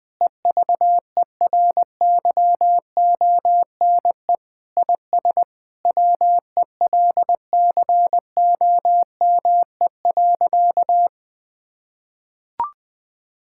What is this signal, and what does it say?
Morse code, heard as a keyed tone, about 20 wpm, 700 Hz. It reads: EVERYONE IS WELCOME.
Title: Sentences with 7-Word Phrases - 20wpm